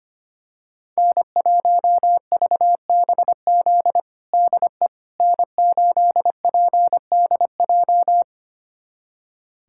Morse code, heard as a keyed tone, about 25 wpm, 700 Hz. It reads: N1VBZ DE N8PDJ
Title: USA Callsign DE Callsign - 5 Characters - 25wpm